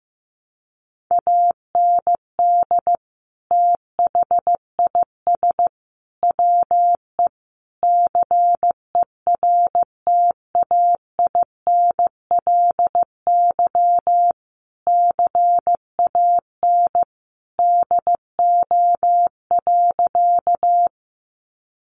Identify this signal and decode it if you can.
Morse code, heard as a keyed tone, about 15 wpm, 700 Hz. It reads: AND THIS WE CERTAINLY CAN DO.
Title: Sentences from Top 900 Words 15wpm